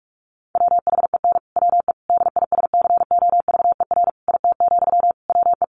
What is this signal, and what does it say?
Morse code, heard as a keyed tone, about 44 wpm, 700 Hz. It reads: WHEN WE DISCOVER IT, WE
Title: Sentences from Top 1100 Words 45wpm